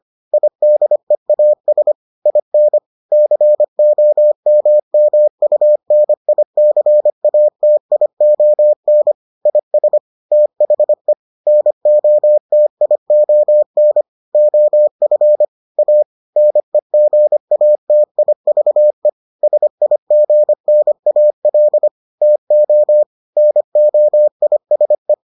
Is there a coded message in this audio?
Yes: Morse code from 0.3 to 25.1 s, heard as a keyed tone, about 25 wpm, 600 Hz. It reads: IDEAS IN COMMUNICATION IS THE NOTION OF A NEGATIVE SIGNAL TO NOISE